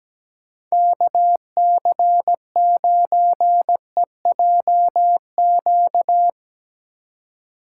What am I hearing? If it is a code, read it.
Morse code, heard as a keyed tone, about 19 wpm, 700 Hz. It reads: KC9EJQ
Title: Speed Racing - USA Call Signs 17wpm